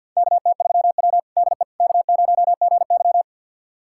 Morse code, heard as a keyed tone, about 48 wpm, 700 Hz. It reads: KT3W DE K0GY
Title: USA Callsign DE Callsign - 4 Characters - 50wpm